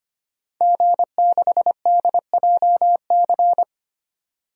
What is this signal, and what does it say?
Morse code, heard as a keyed tone, about 25 wpm, 700 Hz. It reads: G6DJC
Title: DXCC-100 Call Signs - 5 Characters - 25wpm